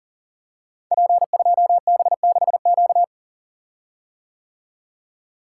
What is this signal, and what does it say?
Morse code, heard as a keyed tone, about 38 wpm, 700 Hz. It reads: P2B6Q
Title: Code Groups - 5 Letter:Numbers - 40wpm